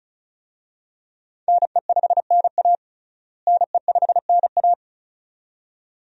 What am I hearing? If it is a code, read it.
Morse code, heard as a keyed tone, about 36 wpm, 700 Hz. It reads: NE5NA NE5NA
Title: USA Call Signs - 5 Characters - Repeated 2x - 35wpm